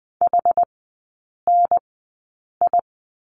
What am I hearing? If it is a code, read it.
Morse code, heard as a keyed tone, about 20 wpm, spaced as for 8 wpm, 700 Hz. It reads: HNI
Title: 100 Lesson 17e - Letter u - Three Characters 20wpm